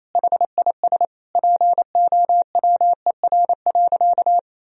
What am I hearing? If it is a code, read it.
Morse code, heard as a keyed tone, about 28 wpm, 700 Hz. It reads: HIS POWER.
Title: Sentences from Top 1000 Words - Review Part 2 - 28wpm